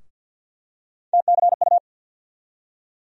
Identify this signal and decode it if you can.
Morse code, heard as a keyed tone, about 51 wpm, 700 Hz. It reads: TCU
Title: USA Call Sign Postfixes - 3 Letters 50wpm